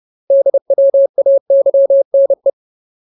Morse code, heard as a keyed tone, about 30 wpm, 550 Hz. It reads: DWAYNE
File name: Ultimate Word List / 30WPM Ultimate